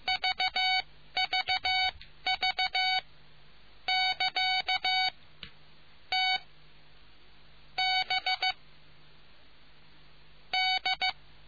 Morse code, heard as a keyed tone, about 15 wpm, spaced as for 10 wpm, 2250 Hz. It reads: VVV<KA> T B D